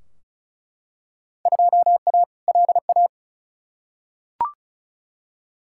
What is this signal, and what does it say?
Morse code, heard as a keyed tone, about 36 wpm, 700 Hz. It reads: 2A LA
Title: Field Day Exchanges Encore - 35wpm